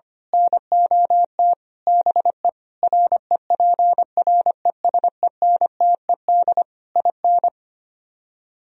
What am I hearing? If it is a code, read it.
Morse code, heard as a keyed tone, about 24 wpm, 700 Hz. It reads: NOT BE REPRESENTED IN
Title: Speed-Racing - Sets of 5 Words 25wpm